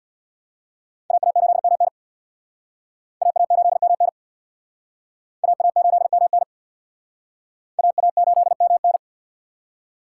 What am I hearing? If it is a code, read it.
Morse code, heard as a keyed tone, about 62 wpm, 700 Hz. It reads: AA8MN AA8MN AA8MN AA8MN
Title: Speed Racing - USA 2x2 Call Signs 50wpm